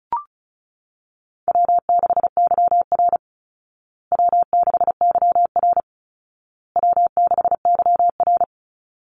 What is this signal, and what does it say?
Morse code, heard as a keyed tone, about 34 wpm, 700 Hz. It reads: W6YR W6YR W6YR
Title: Easy Call Signs Repeated 3x - 35wpm